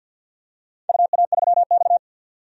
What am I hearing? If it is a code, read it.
Morse code, heard as a keyed tone, about 52 wpm, 700 Hz. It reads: UA3X
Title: DXCC-100 Call Signs - 4 Characters - 50wpm